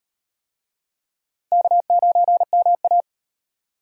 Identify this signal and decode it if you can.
Morse code, heard as a keyed tone, about 38 wpm, 700 Hz. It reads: K9MA